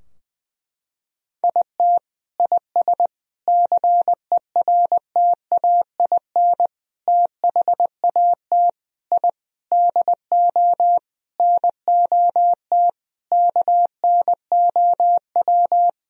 Morse code, heard as a keyed tone, about 21 wpm, 700 Hz. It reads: IT IS CERTAIN THAT I DO NOT KNOW
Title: Sentences from Top 1000 Words - Review Part 2 - 20wpm